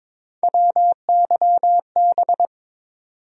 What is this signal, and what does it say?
Morse code, heard as a keyed tone, about 22 wpm, 700 Hz. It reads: WYB
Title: USA Call Sign Postfixes - 3 Letters 22wpm